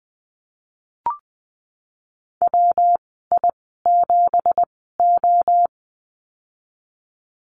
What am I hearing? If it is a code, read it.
Morse code, heard as a keyed tone, about 20 wpm, spaced as for 15 wpm, 700 Hz. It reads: WI7O
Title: 167 Lesson 27g - Number 7 - Call Signs 20wpm